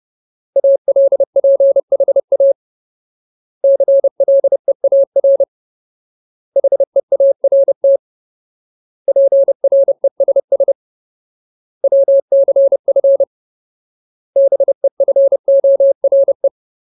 Morse code, heard as a keyed tone, about 30 wpm, 550 Hz. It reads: ALPHA CLEAR HEART PRESS WCF BEFORE
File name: Ultimate Word List / 30WPM Ultimate